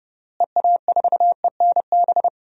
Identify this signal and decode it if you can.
Morse code, heard as a keyed tone, about 30 wpm, 700 Hz. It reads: EA4ENB
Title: DXCC-100 Call Signs - 6 Characters - Repeated 3x - 30wpm